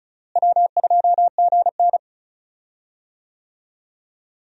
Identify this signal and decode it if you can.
Morse code, heard as a keyed tone, about 36 wpm, 700 Hz. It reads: W2GN
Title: USA Call Signs - 4 Characters - Repeated 2x - 35wpm